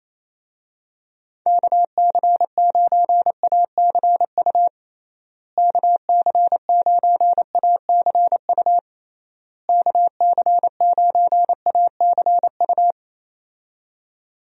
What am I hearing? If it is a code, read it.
Morse code, heard as a keyed tone, about 28 wpm, 700 Hz. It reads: KC9ACU KC9ACU KC9ACU
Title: USA Call Signs - 6 Characters - Repeated 3x - 28wpm